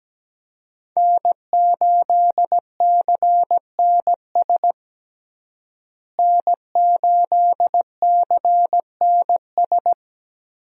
Morse code, heard as a keyed tone, about 17 wpm, 700 Hz. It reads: N8CNS N8CNS
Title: USA Call Signs - 5 Characters - Repeated 3x - 17wpm